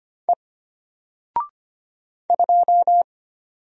Morse code, heard as a keyed tone, about 25 wpm, 700 Hz. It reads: E 2
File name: Single-Letter-Number-25wpm